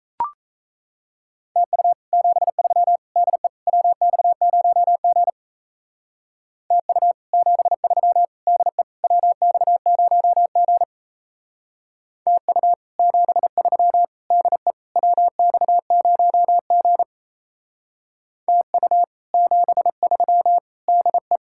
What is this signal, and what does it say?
Morse code, heard as a keyed tone, about 32 wpm, 700 Hz. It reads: TU 73 DE WX0G TU 73 DE WX0G TU 73 DE WX0G TU 73 DE
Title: Speed-Racing Common QSO Phrases - 28wpm